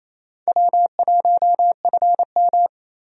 Morse code, heard as a keyed tone, about 28 wpm, 700 Hz. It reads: W1FM
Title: USA Call Signs - 4 Characters - 28wpm